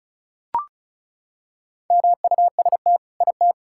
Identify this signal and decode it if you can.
Morse code, heard as a keyed tone, about 36 wpm, 700 Hz. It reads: MUST IT
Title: Sentences from Top 1000 Words - Review Part 5 - 35wpm